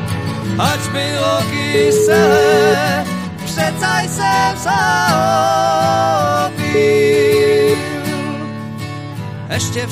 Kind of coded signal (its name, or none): telephone call progress tone